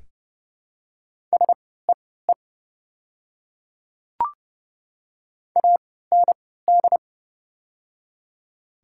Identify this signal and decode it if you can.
Morse code, heard as a keyed tone, about 30 wpm, spaced as for 17 wpm, 700 Hz. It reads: SEE AND
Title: 3x Character Spacing - Top 200 Words - 30wpm